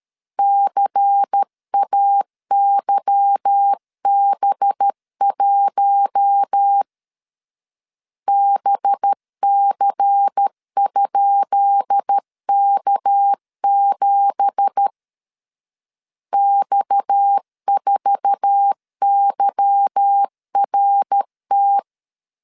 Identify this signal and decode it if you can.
Morse code, heard as a keyed tone, about 13 wpm, 800 Hz. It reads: CAYB1 BC?K7 X4YRT
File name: LEZIONI / 10 SIGNS / 10B INT 12wpm 2414